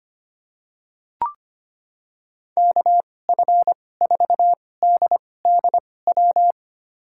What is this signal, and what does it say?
Morse code, heard as a keyed tone, about 25 wpm, spaced as for 18 wpm, 700 Hz. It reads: KF4DDW
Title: USA Call Signs - 6 Characters - 2x Character Spacing - 25wpm